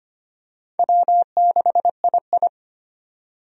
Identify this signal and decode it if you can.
Morse code, heard as a keyed tone, about 25 wpm, 700 Hz. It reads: W6II
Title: USA Call Signs - 4 Characters - 25wpm